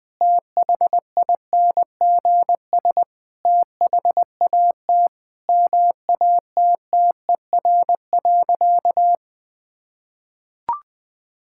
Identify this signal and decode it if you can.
Morse code, heard as a keyed tone, about 21 wpm, 700 Hz. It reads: THINGS THAT MATTER.
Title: Sentences with 4-Word Phrases - Spoken Beginning - 20wpm